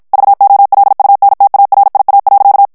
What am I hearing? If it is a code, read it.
Morse code, heard as a keyed tone, about 56 wpm, 800 Hz. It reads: VOLUNTARIA.
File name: ESPAGNOL TEXTE EN CLAIR 1 VITESSE 50 MOTS MINUTES